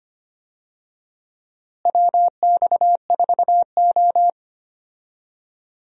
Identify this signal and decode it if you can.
Morse code, heard as a keyed tone, about 25 wpm, 700 Hz. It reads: WX4O
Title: USA Call Signs - 4 Characters - Repeated 2x - 25wpm